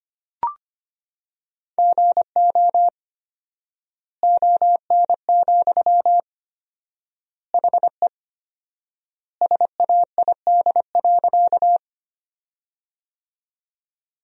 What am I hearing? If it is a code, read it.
Morse code, heard as a keyed tone, about 25 wpm, 700 Hz. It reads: GO ON, HE SAID.
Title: 4x Word-Spacing - Sentences from Top 100 Words - 25wpm